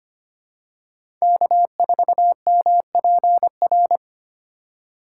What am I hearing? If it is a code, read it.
Morse code, heard as a keyed tone, about 25 wpm, 700 Hz. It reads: K4MPR
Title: USA Call Signs - 5 Characters - 2x Character Spacing - 25wpm